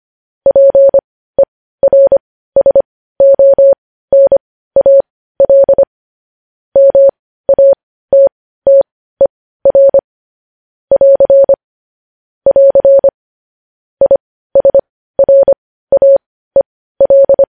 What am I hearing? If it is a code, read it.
Morse code, heard as a keyed tone, about 25 wpm, spaced as for 15 wpm, 550 Hz. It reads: PERSONAL MATTER <AR> <AR> ISRAEL
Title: News Headlines at 15